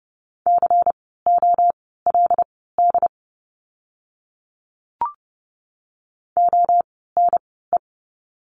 Morse code, heard as a keyed tone, about 30 wpm, spaced as for 17 wpm, 700 Hz. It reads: COLD ONE
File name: Lesson 16f - Letter c - Words 30wpm